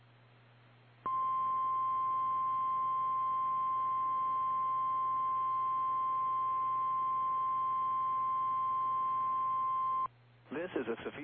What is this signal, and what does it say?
Emergency Alert System (SAME) tones (the 1050 Hz attention tone), with speech. It is heard from 1.0 to 10.1 s.
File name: eas / KKPRKXPN / KKPRKXPN